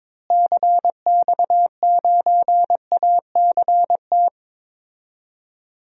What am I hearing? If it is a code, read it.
Morse code, heard as a keyed tone, about 22 wpm, 700 Hz. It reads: CX9ACT
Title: DXCC-100 Call Signs - 6 Characters - Repeated 2x - 22wpm